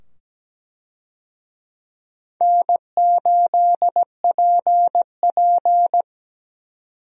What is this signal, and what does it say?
Morse code, heard as a keyed tone, about 17 wpm, 700 Hz. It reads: N8PP
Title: USA Call Signs - 4 Characters - 17wpm